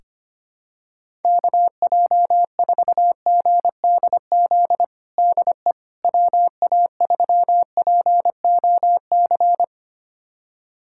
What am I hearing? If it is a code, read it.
Morse code, heard as a keyed tone, about 25 wpm, 700 Hz. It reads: KJ4GDZ DE WA3POC
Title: USA Callsign DE Callsign 25wpm